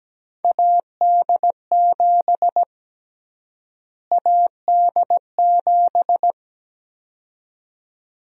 Morse code, heard as a keyed tone, about 17 wpm, 700 Hz. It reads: AD7 AD7
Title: USA Call Sign Prefixes - Repeated 2x - 17wpm